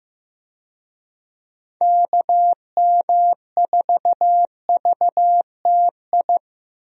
Morse code, heard as a keyed tone, about 16 wpm, 700 Hz. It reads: KM4VTI
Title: USA Call Signs - 6 Characters - Repeated 2x - 15wpm